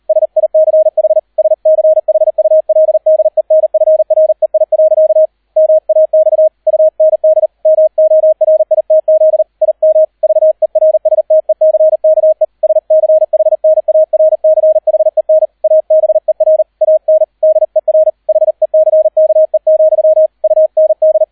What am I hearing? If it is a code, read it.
Morse code, heard as a keyed tone, about 38 wpm, 630 Hz, with radio noise. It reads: SICH SCHULDENFREI. MAX UND MORITZ IM VERSTECKE SCHNARCHEN ABER AN DER HECKE, UND